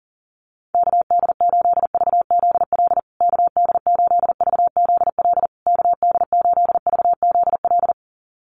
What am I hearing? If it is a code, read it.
Morse code, heard as a keyed tone, about 37 wpm, 700 Hz. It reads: KD8VZL KD8VZL KD8VZL
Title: USA Call Signs Repeated 3x 40wpm